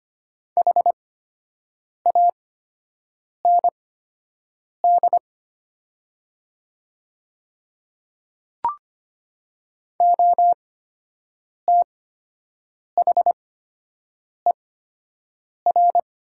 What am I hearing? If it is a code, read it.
Morse code, heard as a keyed tone, about 25 wpm, spaced as for 7 wpm, 700 Hz. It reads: HAND OTHER
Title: 8x Character Spacing - Top 300 Words - 25wpm